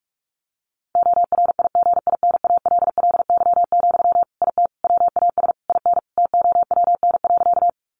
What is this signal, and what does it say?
Morse code, heard as a keyed tone, about 44 wpm, 700 Hz. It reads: ORIGINALLY, IT WAS IN TOWN.
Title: Sentences from Top 1500 Words 45wpm